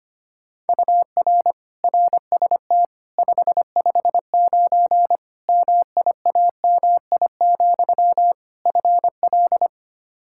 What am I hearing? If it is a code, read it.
Morse code, heard as a keyed tone, about 25 wpm, 700 Hz. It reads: UR RST 559 MIAMI, FL
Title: K1USN SST QSO Practice 25wpm